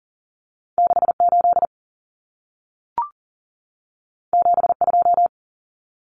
Morse code, heard as a keyed tone, about 40 wpm, 700 Hz. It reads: 68 72
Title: Common Cut Numbers - 2-Digit 40wpm